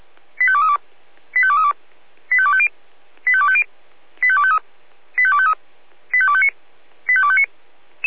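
Five-tone selective call (ZVEI), heard as quick runs of five tones, tones of about 70 ms. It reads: 85232 85232 85250 85250 85253 85253 85259 85259
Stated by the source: ZVEI five-tone selective call received by SDR.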